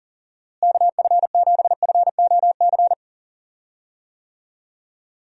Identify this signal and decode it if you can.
Morse code, heard as a keyed tone, about 40 wpm, 700 Hz. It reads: KF7FOC
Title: USA Call Signs - 6 Characters - Repeated 2x - 40wpm